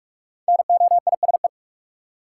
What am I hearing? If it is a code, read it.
Morse code, heard as a keyed tone, about 46 wpm, 700 Hz. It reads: NOISE
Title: Five Letter Words - 45wpm